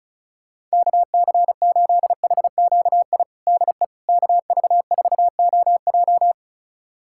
Morse code, heard as a keyed tone, about 36 wpm, 700 Hz. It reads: KC8HQI DE KV4OJ
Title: USA Callsign DE Callsign 35wpm